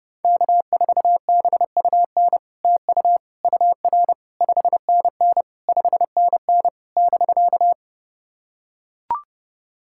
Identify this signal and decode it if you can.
Morse code, heard as a keyed tone, about 30 wpm, 700 Hz. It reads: K4BUN TU UR 5NN 5NN <BK>
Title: SOTA QSO Practice 30wpm